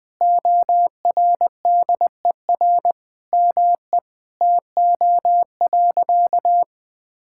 Morse code, heard as a keyed tone, about 20 wpm, 700 Hz. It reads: ORDER ME TO.
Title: Sentences from Top 1000 Words - Review Part 6 - 20wpm